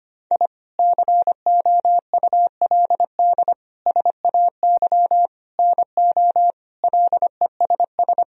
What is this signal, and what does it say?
Morse code, heard as a keyed tone, about 25 wpm, 700 Hz. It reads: I COULD SAY NO LESS
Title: Sentences from Top 1000 Words - Review Part 2 - 25wpm